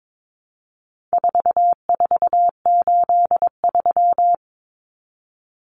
Morse code, heard as a keyed tone, about 22 wpm, 700 Hz. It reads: 4483
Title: Common Cut Numbers - 4-Digit 22wpm